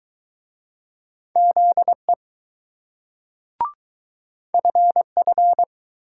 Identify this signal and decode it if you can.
Morse code, heard as a keyed tone, about 20 wpm, 700 Hz. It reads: ZE FF